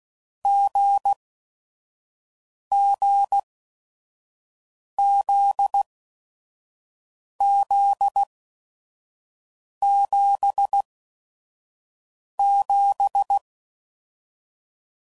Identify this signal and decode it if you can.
Morse code, heard as a keyed tone, about 16 wpm, spaced as for 5 wpm, 800 Hz. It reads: GGZZ77
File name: Combinations / Difficult Combos / GZ GZ7